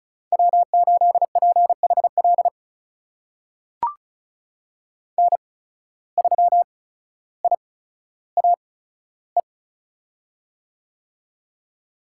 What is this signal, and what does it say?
Morse code, heard as a keyed tone, about 36 wpm, spaced as for 15 wpm, 700 Hz. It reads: W8PHL N3IAE
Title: USA Call Signs - 5 Characters - 8x Character Spacing - 35wpm